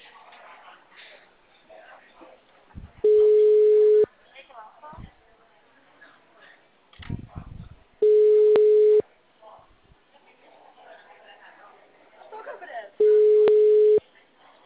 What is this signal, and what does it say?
Ringback tone (Europe, 425 Hz), with speech.
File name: out- -2001- - - .7595